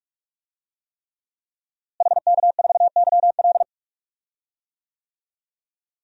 Morse code, heard as a keyed tone, about 46 wpm, 700 Hz. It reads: HK4YL
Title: DXCC-100 Call Signs - 5 Characters - Repeated 3x - 45wpm